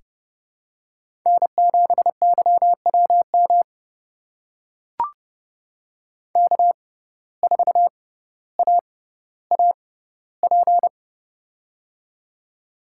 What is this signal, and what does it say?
Morse code, heard as a keyed tone, about 32 wpm, spaced as for 16 wpm, 700 Hz. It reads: N7YWM K4AAP
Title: USA Call Signs - 5 Characters - 6x Character Spacing - 30wpm